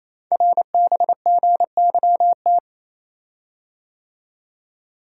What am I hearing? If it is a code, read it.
Morse code, heard as a keyed tone, about 27 wpm, 700 Hz. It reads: RBGYT
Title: Code Groups - 5 Letters - 28wpm